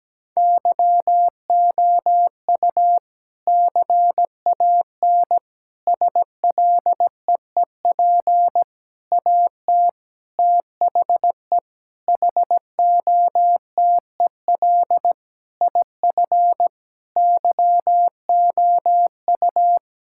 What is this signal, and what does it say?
Morse code, heard as a keyed tone, about 17 wpm, 700 Hz. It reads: YOU CAN SLEEP AT THE HOTEL IF YOU
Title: Sentences from Top 1600 Words 17wpm